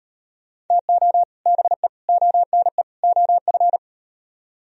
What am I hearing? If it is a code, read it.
Morse code, heard as a keyed tone, about 38 wpm, 700 Hz. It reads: TO BE ONE OF